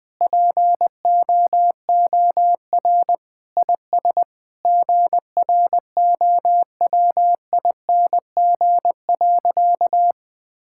Morse code, heard as a keyed tone, about 20 wpm, 700 Hz. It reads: POOR IS GROWING.
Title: Sentences with 8-Word Phrases - 20wpm